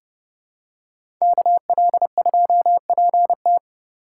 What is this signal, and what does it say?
Morse code, heard as a keyed tone, about 30 wpm, 700 Hz. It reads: KL2PT
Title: USA Call Signs - 5 Characters - 3x Character Spacing - 30wpm